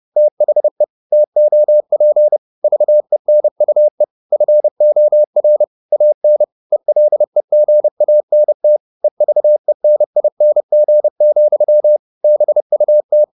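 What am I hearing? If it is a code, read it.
Morse code, heard as a keyed tone, about 30 wpm, 600 Hz. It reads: THE TOP VENUE FOR AN ELEGANT EVENING, BUT
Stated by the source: Morse code at 30 wpm, the novel Never by Ken Follett in English, from CW med Gnister.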